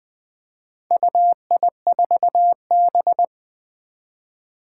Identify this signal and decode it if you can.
Morse code, heard as a keyed tone, about 21 wpm, 700 Hz. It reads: UI4B
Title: Four Challenging Letter-Number Combinations 20wpm